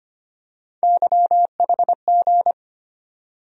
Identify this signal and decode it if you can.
Morse code, heard as a keyed tone, about 25 wpm, 700 Hz. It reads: YHG
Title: USA Call Sign Postfixes - 3 Letters 25wpm